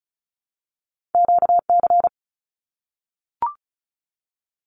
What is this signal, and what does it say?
Morse code, heard as a keyed tone, about 36 wpm, 700 Hz. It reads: QC
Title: Canadian Provinces and Territories 35wpm